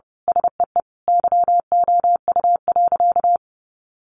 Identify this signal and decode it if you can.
Morse code, heard as a keyed tone, about 30 wpm, 700 Hz. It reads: SEE YOU.
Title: Sentences from Top 800 Words 30wpm